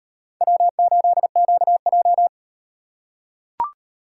Morse code, heard as a keyed tone, about 38 wpm, 700 Hz. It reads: W8QJ